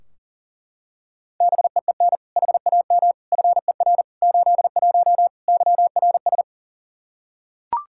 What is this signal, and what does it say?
Morse code, heard as a keyed tone, about 40 wpm, 700 Hz. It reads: BEEN HAM FER 81 YRS
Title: Practice QSO - Exchange 3 - More Info - 40wpm